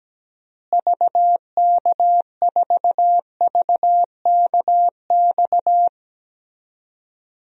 Morse code, heard as a keyed tone, about 17 wpm, 700 Hz. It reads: VK4VKX
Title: DXCC-100 Call Signs - 6 Characters - Repeated 2x - 17wpm